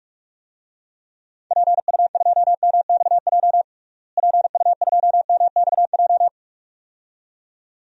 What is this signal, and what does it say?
Morse code, heard as a keyed tone, about 41 wpm, 700 Hz. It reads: PU2MXJ PU2MXJ
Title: DXCC-100 Call Signs - 6 Characters - Repeated 2x - 45wpm